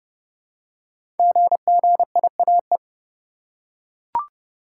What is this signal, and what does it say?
Morse code, heard as a keyed tone, about 29 wpm, 700 Hz. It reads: GGIAE
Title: Code Groups - 5 Letters - 30wpm